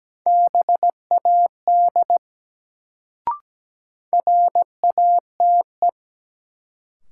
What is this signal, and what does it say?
Morse code, heard as a keyed tone, about 17 wpm, 700 Hz. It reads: BAD RATE
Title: Sentences from Top 700 Words - New Word Drill - 17wpm